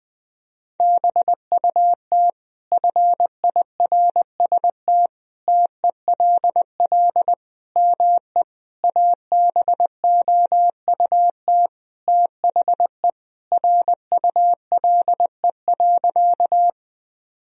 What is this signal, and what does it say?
Morse code, heard as a keyed tone, about 20 wpm, 700 Hz. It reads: BUT FIRST TELL ME ABOUT THE RULE.
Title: Sentences from Top 1000 Words - Review Part 2 - 20wpm